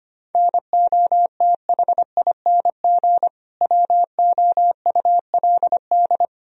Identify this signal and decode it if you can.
Morse code, heard as a keyed tone, about 25 wpm, 700 Hz. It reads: NOTHING WOULD
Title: Sentences from Top 1000 Words - Review Part 3 - 25wpm